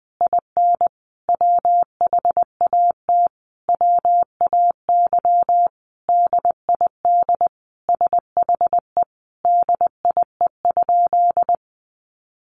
Morse code, heard as a keyed tone, about 19 wpm, 700 Hz. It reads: IN WHAT WAY DID SHE DIE?
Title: Sentences from Top 1300 Words 20wpm